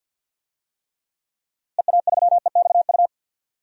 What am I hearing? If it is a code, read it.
Morse code, heard as a keyed tone, about 46 wpm, 700 Hz. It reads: EA3EXU
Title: DXCC-100 Call Signs - 6 Characters - Repeated 2x - 50wpm